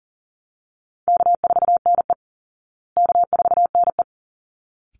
Morse code, heard as a keyed tone, about 37 wpm, 700 Hz. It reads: K4NE K4NE
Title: Easy Call Signs Repeated 2x - 40wpm